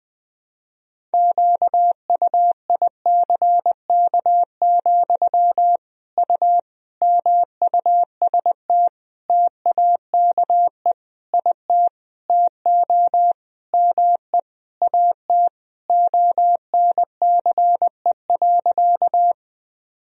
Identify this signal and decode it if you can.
Morse code, heard as a keyed tone, about 20 wpm, 700 Hz. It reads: QUICK, U MUST TAKE IT TO ME AT ONCE.
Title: Sentences from Top 500 Words and Abbreviations 20wpm